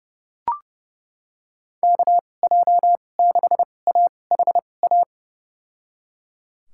Morse code, heard as a keyed tone, about 30 wpm, spaced as for 22 wpm, 700 Hz. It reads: KJ6AHA
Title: USA Call Signs - 2x Character Spacing - 30wpm